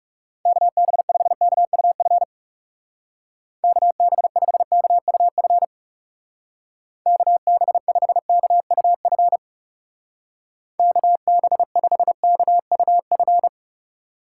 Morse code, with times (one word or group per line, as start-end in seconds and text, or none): 0.5-2.2 KB5KUF
3.6-5.6 KB5KUF
7.1-9.4 KB5KUF
10.8-13.5 KB5KUF